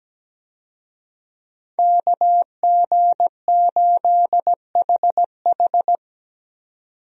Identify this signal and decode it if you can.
Morse code, heard as a keyed tone, about 17 wpm, 700 Hz. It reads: KG8HH
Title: USA Call Signs - 5 Characters - 17wpm